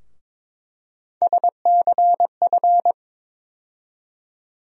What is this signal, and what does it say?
Morse code, heard as a keyed tone, about 22 wpm, 700 Hz. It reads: SCF
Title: USA Call Sign Postfixes - 3 Letters 22wpm